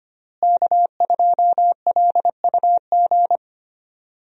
Morse code, heard as a keyed tone, about 25 wpm, 700 Hz. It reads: K2LUG
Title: USA Call Signs - 5 Characters - Repeated 3x - 25wpm